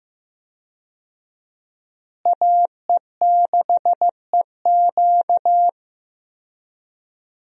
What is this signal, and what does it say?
Morse code, heard as a keyed tone, about 15 wpm, 700 Hz. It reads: AE6EQ
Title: USA Call Signs - 5 Characters - Repeated 3x - 15wpm